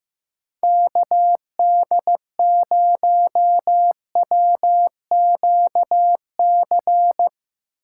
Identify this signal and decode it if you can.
Morse code, heard as a keyed tone, about 15 wpm, 700 Hz. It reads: KD0WQC
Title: Speed Racing - USA 2x3 Call Signs 15wpm